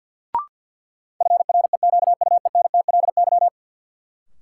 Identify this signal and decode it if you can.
Morse code, heard as a keyed tone, about 48 wpm, 700 Hz. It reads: FREQUENTLY
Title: Sentences from Top 1300 Words - New Word Drill - 50wpm